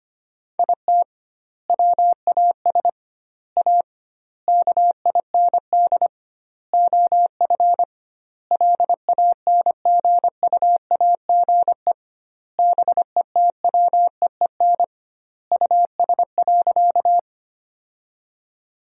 Morse code, heard as a keyed tone, about 25 wpm, 700 Hz. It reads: IT WAS A KIND OF LANGUAGE BETWEEN US.
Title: 2x Word-Spacing - Sentences from Top 500 Words - 25wpm